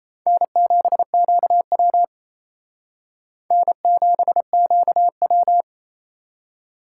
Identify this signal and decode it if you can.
Morse code, heard as a keyed tone, about 35 wpm, 700 Hz. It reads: N7QW N7QW